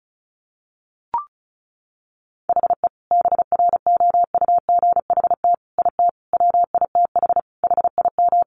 Morse code, heard as a keyed tone, about 34 wpm, 700 Hz. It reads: HE BROUGHT IT WITH HIM
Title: Sentences from Top 500 Words 35wpm